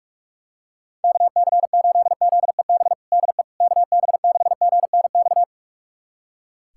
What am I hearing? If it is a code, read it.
Morse code, heard as a keyed tone, about 46 wpm, 700 Hz. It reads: KC8ZEB DE KB6GNX
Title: USA Callsign DE Callsign 45wpm